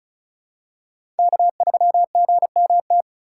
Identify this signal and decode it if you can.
Morse code, heard as a keyed tone, about 36 wpm, 700 Hz. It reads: K3GMT
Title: K1USN SST QSO Practice - 35wpm